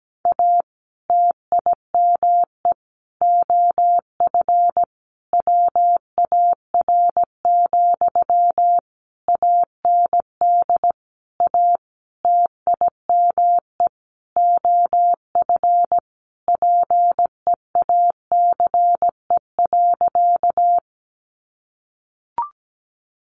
Morse code, heard as a keyed tone, about 17 wpm, 700 Hz. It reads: A TIME OF WAR, AND A TIME OF PEACE.
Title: Sentences from Top 1500 Words 17wpm